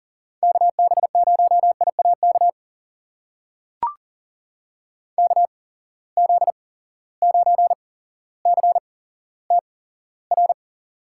Morse code, heard as a keyed tone, about 39 wpm, spaced as for 17 wpm, 700 Hz. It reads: KB0IAK KZ9CTR
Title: USA Call Signs - 6 Characters - 8x Character Spacing - 40wpm